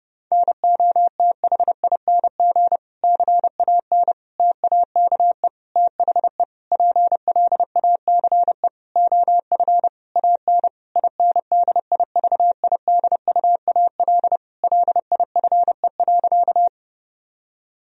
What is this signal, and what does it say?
Morse code, heard as a keyed tone, about 32 wpm, 700 Hz. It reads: NOTHING CAN TAKE THE PLACE OF AN INDIVIDUAL LIFE.
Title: Sentences from Top 1000 Words - Review Part 4 - 30wpm